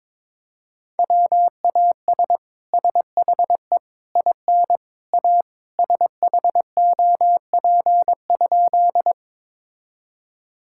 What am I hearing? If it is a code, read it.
Morse code, heard as a keyed tone, about 22 wpm, 700 Hz. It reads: WAS SHE IN A SHOP?
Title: Sentences from Top 1600 Words 22wpm